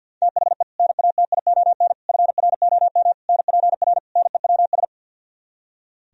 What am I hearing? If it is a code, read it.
Morse code, heard as a keyed tone, about 64 wpm, 700 Hz. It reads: THE NATION FROM NPR NEWS